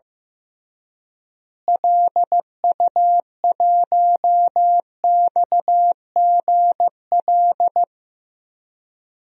Morse code, heard as a keyed tone, about 16 wpm, 700 Hz. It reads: LU1XGL
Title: DXCC-100 Call Signs - 6 Characters - Repeated 3x - 15wpm